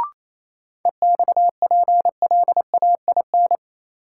Morse code, heard as a keyed tone, about 28 wpm, 700 Hz. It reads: EXPLAIN
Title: Sentences from Top 1100 Words - New Word Drill Alt - 28wpm